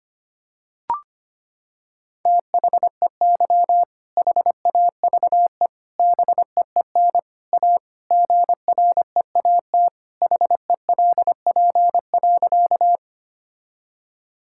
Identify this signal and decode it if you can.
Morse code, heard as a keyed tone, about 25 wpm, 700 Hz. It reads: THEY HAVE BEEN A GREAT HELP.
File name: practice / 25wpm / Sentences with 3-Word Phrases - 25wpm